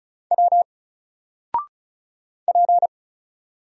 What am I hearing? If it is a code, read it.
Morse code, heard as a keyed tone, about 36 wpm, 700 Hz. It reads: W P
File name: Single-Letter-Number-35wpm